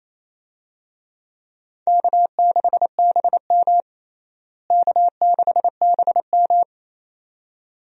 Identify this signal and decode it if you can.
Morse code, heard as a keyed tone, about 28 wpm, 700 Hz. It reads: K6BM K6BM
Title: USA Call Signs - 4 Characters - Repeated 2x - 28wpm